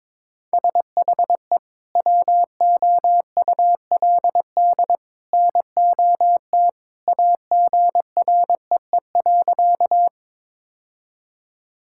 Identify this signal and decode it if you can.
Morse code, heard as a keyed tone, about 22 wpm, 700 Hz. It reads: SHE WOULD NOT AGREE.
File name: practice / 22wpm / Sentences with 2-Word Phrases - 22wpm